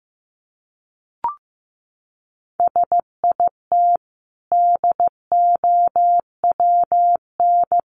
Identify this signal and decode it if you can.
Morse code, heard as a keyed tone, about 15 wpm, 700 Hz. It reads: SIT DOWN
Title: Sentences from Top 500 Words 15wpm